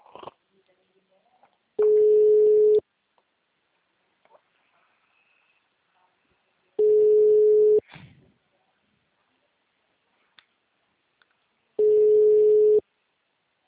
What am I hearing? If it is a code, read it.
Ringback tone (Europe, 425 Hz), with speech.